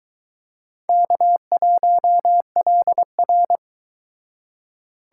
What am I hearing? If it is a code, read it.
Morse code, heard as a keyed tone, about 23 wpm, 700 Hz. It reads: K1LR